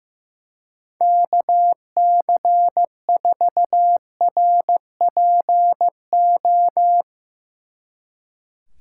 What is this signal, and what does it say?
Morse code, heard as a keyed tone, about 16 wpm, 700 Hz. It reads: KC4RPO
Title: Speed Racing - USA 2x3 Call Signs 15wpm